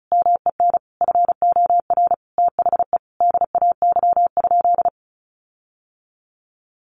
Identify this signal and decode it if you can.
Morse code, heard as a keyed tone, about 34 wpm, 700 Hz. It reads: MEN FOR THE DAY?
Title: Sentences from Top 1400 Words 35wpm